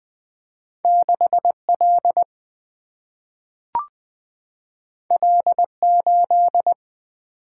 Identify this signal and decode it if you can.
Morse code, heard as a keyed tone, about 20 wpm, 700 Hz. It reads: 6L L8
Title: Single-Letter Single-Number 20wpm